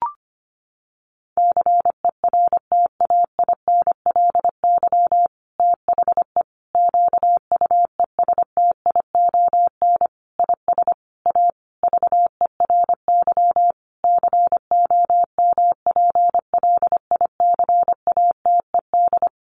Morse code, heard as a keyed tone, about 25 wpm, 700 Hz. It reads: CERTAINLY THE QUESTION IS A VERY COMPLICATED
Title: Sentences from Top 1200 Words 25wpm